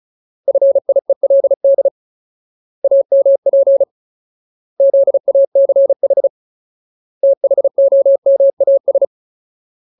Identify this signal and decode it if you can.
Morse code, heard as a keyed tone, about 35 wpm, 550 Hz. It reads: FIELD AMP ZACH THOMAS
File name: Ultimate Word List / 35WPM Ultimate